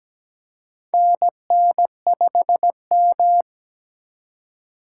Morse code, heard as a keyed tone, about 17 wpm, 700 Hz. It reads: NN5M
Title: USA Call Signs - 4 Characters - 6x Character Spacing - 17wpm